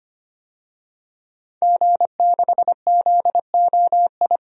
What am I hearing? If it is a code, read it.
Morse code, heard as a keyed tone, about 25 wpm, 700 Hz. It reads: G6ZOI
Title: DXCC-100 Call Signs - 5 Characters - 25wpm